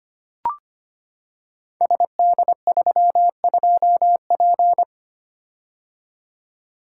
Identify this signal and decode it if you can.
Morse code, heard as a keyed tone, about 25 wpm, 700 Hz. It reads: SD32P
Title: Code Groups - 5 Letter:Numbers - 25wpm